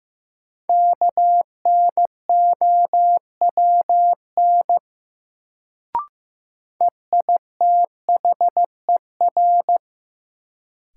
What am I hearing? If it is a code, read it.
Morse code, heard as a keyed tone, about 15 wpm, 700 Hz. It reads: KNOWN EITHER
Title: Sentences from Top 600 Words - New Word Drill - 15wpm